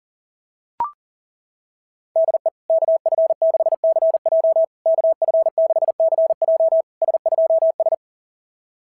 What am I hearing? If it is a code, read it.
Morse code, heard as a keyed tone, about 40 wpm, 660 Hz. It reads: DE KF6CJ KF6CJ S2S
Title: SOTA QSO Practice 40wpm